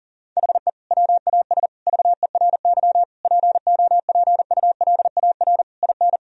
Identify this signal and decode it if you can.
Morse code, heard as a keyed tone, about 37 wpm, 700 Hz. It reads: HE WAS VERY POPULAR IN
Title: Sentences from Top 1000 Words - Review Part 3 - 40wpm